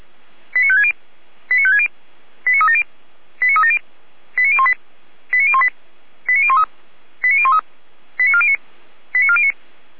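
Five-tone selective call (ZVEI), heard as quick runs of five tones, tones of about 70 ms. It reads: 89560 89560 89370 89370 89018 89018 89013 89013 89409 89409
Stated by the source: ZVEI five-tone call 89391 received by SDR, audio.f85360.feuerwehr-orlamuende.org fire brigade server.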